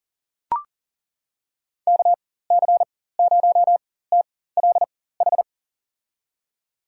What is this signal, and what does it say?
Morse code, heard as a keyed tone, about 38 wpm, spaced as for 18 wpm, 700 Hz. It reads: KC0TLH
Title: USA Call Signs - 6 Characters - 4x Character Spacing - 40wpm